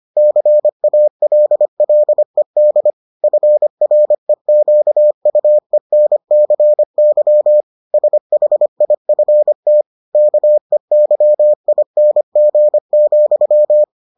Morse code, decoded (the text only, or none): CALLED FREQUENCY SHIFT KEYING,